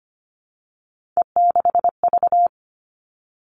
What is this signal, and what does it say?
Morse code, heard as a keyed tone, about 25 wpm, 700 Hz. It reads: E6V
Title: All Cut Numbers - 3-Digit 25wpm